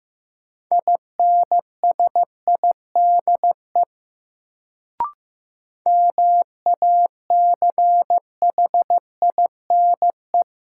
Morse code, heard as a keyed tone, about 15 wpm, 700 Hz. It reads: INSIDE MACHINE
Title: Sentences from Top 1000 Words - Review Part 2 - New Word Drill - 15wpm